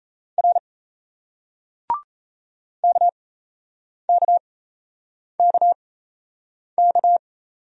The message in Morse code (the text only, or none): R KKKK